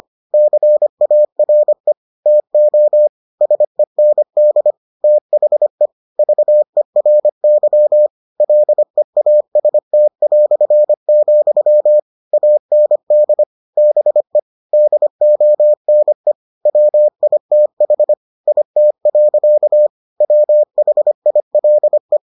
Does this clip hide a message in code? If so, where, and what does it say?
Yes: Morse code from 0.3 to 22.2 s, heard as a keyed tone, about 25 wpm, 600 Hz. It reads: CARE TO SEND THE VERY LEAST", AND BE DONE WITH IT. WHILE